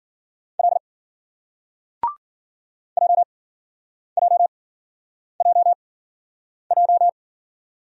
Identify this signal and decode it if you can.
Morse code, heard as a keyed tone, about 54 wpm, spaced as for 9 wpm, 700 Hz. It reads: L JJJJ